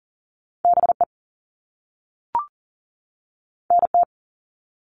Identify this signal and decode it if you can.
Morse code, heard as a keyed tone, about 40 wpm, 700 Hz. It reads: BE NT
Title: All Cut Numbers - 2-Digit 40wpm